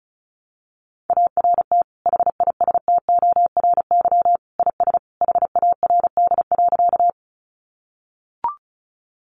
Morse code, heard as a keyed tone, about 34 wpm, 700 Hz. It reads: ART HISTORY IS HARD.
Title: Sentences from Top 600 Words 35wpm